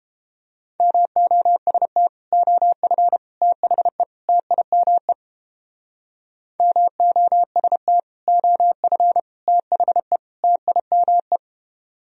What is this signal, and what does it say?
Morse code, heard as a keyed tone, about 29 wpm, 700 Hz. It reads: MOST OF THE TIME MOST OF THE TIME